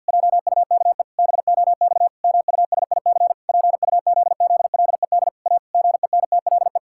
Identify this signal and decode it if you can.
Morse code, heard as a keyed tone, about 50 wpm, 700 Hz. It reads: JUKE BOX MUSIC PUZZLED A GENTLE
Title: Pangram Challenge v2 - 50wpm